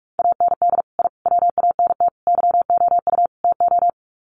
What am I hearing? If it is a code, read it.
Morse code, heard as a keyed tone, about 44 wpm, 700 Hz. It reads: AND I WANT YOU TO
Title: Sets of 5 Words Encore 45wpm